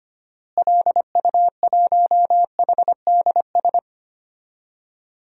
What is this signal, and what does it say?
Morse code, heard as a keyed tone, about 25 wpm, 690 Hz. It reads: LU1HDS
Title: DXCC-100 Call Signs - 6 Characters - Repeated 3x - 25wpm